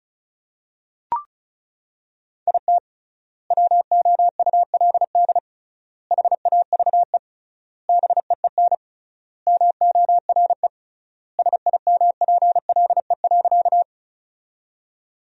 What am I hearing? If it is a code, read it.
Morse code, heard as a keyed tone, about 34 wpm, 700 Hz. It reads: IT WOULD HAVE BEEN MORE SIMPLE.
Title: 3x Word-Spacing - Sentences from Top 400 Words - 35wpm